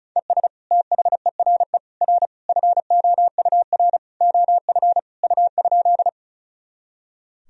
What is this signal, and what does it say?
Morse code, heard as a keyed tone, about 36 wpm, 700 Hz. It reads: ES THERE R FOUR OF U?
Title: Sentences from Top 300 Words and Abbreviations 35wpm